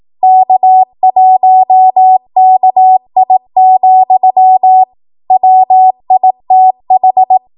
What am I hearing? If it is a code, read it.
Morse code, heard as a keyed tone, about 18 wpm, 750 Hz. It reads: K1KI, WITH